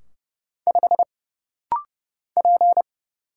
Morse code, heard as a keyed tone, about 30 wpm, 700 Hz. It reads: 5 P
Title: Single Letter-Number - Rapid-Fire 30wpm